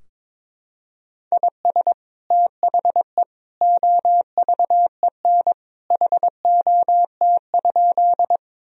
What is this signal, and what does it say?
Morse code, heard as a keyed tone, about 22 wpm, 700 Hz. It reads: IS THE OVEN HOT?
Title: Sentences from Top 1000 Words - Review Part 4 - 22wpm